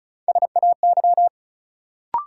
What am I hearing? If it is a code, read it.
Morse code, heard as a keyed tone, about 36 wpm, 700 Hz. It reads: SAY
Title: Sets of 2 Words - New Word Drill - 35wpm